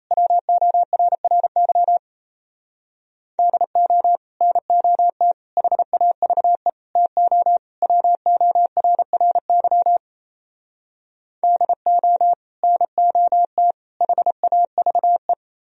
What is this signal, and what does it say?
Morse code, heard as a keyed tone, about 36 wpm, 700 Hz. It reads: WORRY DO NOT HAVE TO WORRY DO NOT HAVE